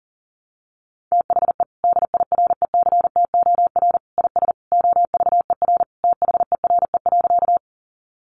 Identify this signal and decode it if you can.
Morse code, heard as a keyed tone, about 40 wpm, 700 Hz. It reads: THE DIRECTOR IS OVER THERE.
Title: Sentences from Top 1300 Words 40wpm